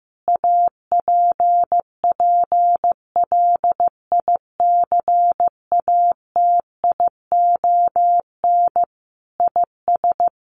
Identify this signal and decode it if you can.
Morse code, heard as a keyed tone, about 14 wpm, 700 Hz. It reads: APPLICATION IS
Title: Sentences from Top 1200 Words 15wpm